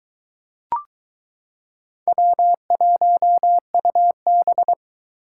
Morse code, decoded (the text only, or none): W1UB